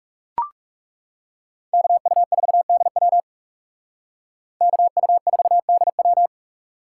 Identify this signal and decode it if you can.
Morse code, heard as a keyed tone, about 37 wpm, 700 Hz. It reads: KU4DW KU4DW
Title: Speed Racing - USA 2x2 Call Signs 30wpm